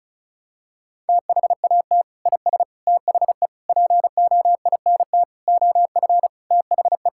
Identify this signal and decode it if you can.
Morse code, heard as a keyed tone, about 36 wpm, 700 Hz. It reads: THAT IS THE POINT OF THE
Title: Sentences from Top 1000 Words - Review Part 5 - 35wpm